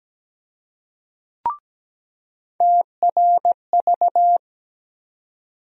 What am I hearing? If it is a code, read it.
Morse code, heard as a keyed tone, about 17 wpm, 700 Hz. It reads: TRV
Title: USA Call Sign Postfixes - 3 Letters 17wpm